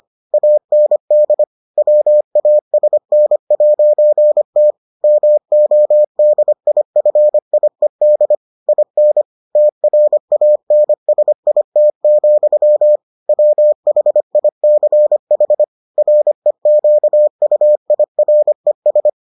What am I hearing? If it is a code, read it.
Morse code, heard as a keyed tone, about 25 wpm, 600 Hz. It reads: AND WASN'T MODIFIED IN TRANSIT, WHICH REQUIRES